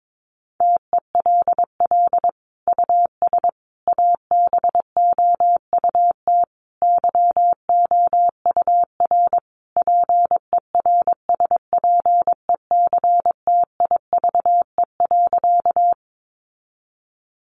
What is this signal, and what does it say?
Morse code, heard as a keyed tone, about 22 wpm, 700 Hz. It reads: TELL US ABOUT YOUR PERSPECTIVE.
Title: Sentences from Top 1200 Words 22wpm